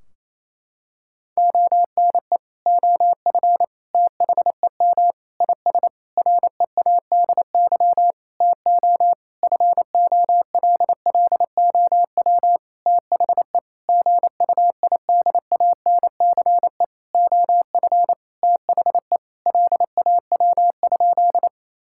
Morse code, heard as a keyed tone, about 28 wpm, 700 Hz. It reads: ONE OF THEM IS READY TO FOLLOW THE GUIDANCE OF THE LAW?
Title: Sentences from Top 1600 Words 28wpm